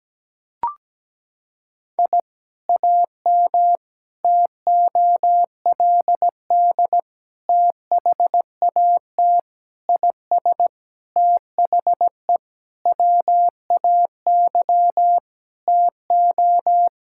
Morse code, heard as a keyed tone, about 17 wpm, 700 Hz. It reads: I AM TOLD THAT IS THE WAY TO
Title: Sentences from Top 1000 Words - Review Part 5 - 17wpm